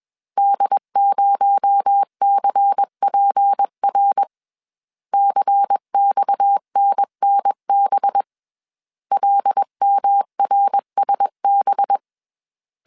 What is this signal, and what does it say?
Morse code, heard as a keyed tone, about 21 wpm, 800 Hz. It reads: D0CPR CXNNB LMRSB
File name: CW / DET / 1 6 1500C S A 20wpm 2048